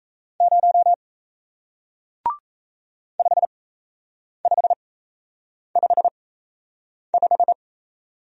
Morse code, heard as a keyed tone, about 38 wpm, spaced as for 8 wpm, 700 Hz. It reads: 0 5555